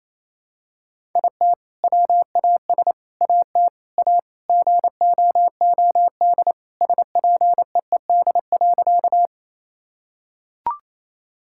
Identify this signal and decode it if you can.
Morse code, heard as a keyed tone, about 28 wpm, 700 Hz. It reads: IT WAS AT A GOOD SPEED.
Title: Sentences from Top 1000 Words - Review Part 5 - 28wpm